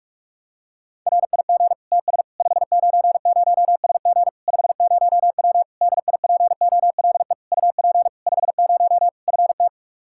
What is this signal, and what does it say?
Morse code, heard as a keyed tone, about 46 wpm, 700 Hz. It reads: RIG TS 590SG 50W DIPOLE UP 50 FT
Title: Practice QSO - Exchange 2 - Info - 45wpm